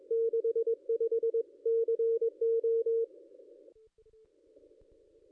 Morse code, heard as a keyed tone, about 22 wpm, 460 Hz. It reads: 65 CO
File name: contests / ss-cw-k5tr-op-n5rz -0400